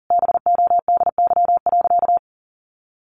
Morse code, heard as a keyed tone, about 40 wpm, 700 Hz. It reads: BODY.
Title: A Sound Of Thunder 40wpm